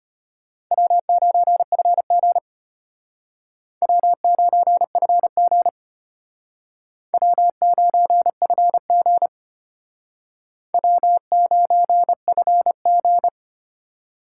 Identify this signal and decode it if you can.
Morse code, heard as a keyed tone, about 32 wpm, 700 Hz. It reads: W9FG W9FG W9FG W9FG